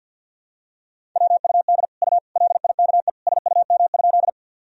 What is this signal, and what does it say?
Morse code, heard as a keyed tone, about 51 wpm, 700 Hz. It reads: WUD U LIKE SUM?
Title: Sentences from Top 100 Words and Abbreviations 50wpm